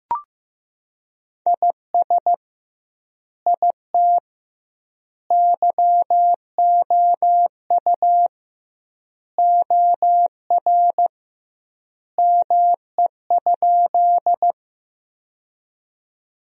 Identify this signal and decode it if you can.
Morse code, heard as a keyed tone, about 16 wpm, 700 Hz. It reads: IS IT YOU OR ME?
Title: 2x Word-Spacing - Sentences from Top 200 Words - 15wpm